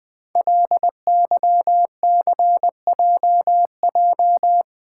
Morse code, heard as a keyed tone, about 20 wpm, 700 Hz. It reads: LYCJJ
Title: Code Groups - 5 Letter:Numbers - 20wpm